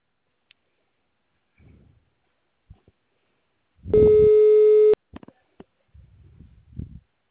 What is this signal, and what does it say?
Ringback tone (Europe, 425 Hz).
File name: out- -2007- - - .67708